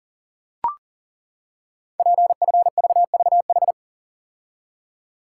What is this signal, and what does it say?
Morse code, heard as a keyed tone, about 42 wpm, 700 Hz. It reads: PFVVH
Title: Code Groups - 5 Letters - 40wpm